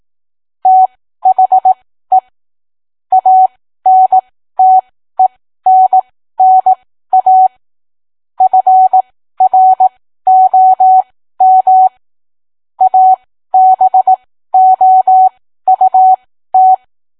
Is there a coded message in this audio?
Yes: Morse code from 0.7 to 16.7 s, heard as a keyed tone, about 18 wpm, spaced as for 13 wpm, 750 Hz. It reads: THE ANTENNA FROM ABOUT